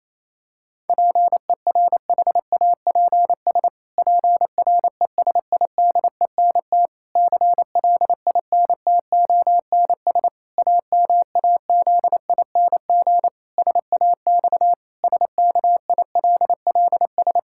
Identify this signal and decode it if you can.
Morse code, heard as a keyed tone, about 28 wpm, 700 Hz. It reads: PERHAPS PRESIDENT CLINTONS AMAZING SAX SKILLS
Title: Pangram Challenge v2 - 28wpm